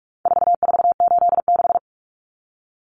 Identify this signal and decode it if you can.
Morse code, heard as a keyed tone, about 44 wpm, 700 Hz. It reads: 4486
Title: Numbers - 4-Digit 45wpm